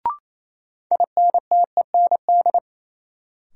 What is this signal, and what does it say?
Morse code, heard as a keyed tone, about 28 wpm, 700 Hz. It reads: INTEND
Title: Sentences from Top 1600 Words - New Word Drill Alt - 28wpm